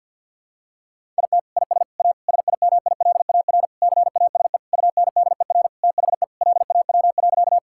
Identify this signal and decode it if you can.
Morse code, heard as a keyed tone, about 51 wpm, 700 Hz. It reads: IT IS A SIMILAR CASE UNDER THE LAW.
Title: Sentences from Top 1000 Words - Review Part 5 - 50wpm